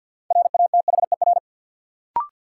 Morse code, heard as a keyed tone, about 53 wpm, 700 Hz. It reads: RATHER
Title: Sentences from Top 700 Words - New Word Drill - 50wpm